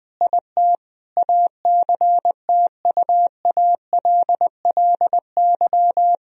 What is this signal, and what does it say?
Morse code, heard as a keyed tone, about 21 wpm, 700 Hz. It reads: IT ACTUALLY
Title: Sentences from Top 1000 Words - Review Part 4 - 20wpm